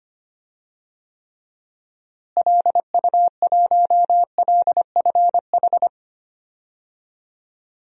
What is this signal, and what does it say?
Morse code, heard as a keyed tone, about 25 wpm, 700 Hz. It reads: LU1LFH
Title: DXCC-100 Call Signs - 6 Characters - Repeated 3x - 25wpm